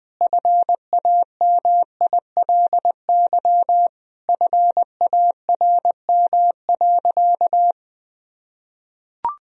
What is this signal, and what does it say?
Morse code, heard as a keyed tone, about 20 wpm, 700 Hz. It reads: FAMILY FARM.
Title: Sentences from Top 1000 Words - Review Part 2 - 20wpm